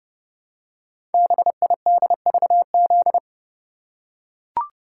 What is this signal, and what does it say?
Morse code, heard as a keyed tone, about 30 wpm, 700 Hz. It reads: BIDVZ
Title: Code Groups - 5 Letter:Numbers - 30wpm